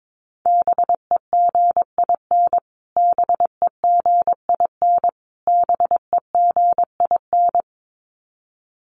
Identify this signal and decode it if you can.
Morse code, heard as a keyed tone, about 22 wpm, 700 Hz. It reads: BEGIN BEGIN BEGIN
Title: Top 500 Words Repeated 22wpm